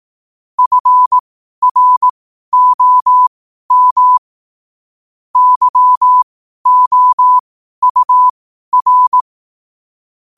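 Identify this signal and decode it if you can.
Morse code, heard as a keyed tone, about 18 wpm, spaced as for 12 wpm, 1000 Hz. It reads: FROM YOUR
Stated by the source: Morse practice sent at 12 wpm.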